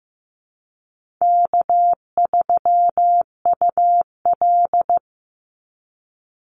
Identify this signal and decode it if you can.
Morse code, heard as a keyed tone, about 15 wpm, 700 Hz. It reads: K3UL
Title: Call Signs Repeated 2x - 15wpm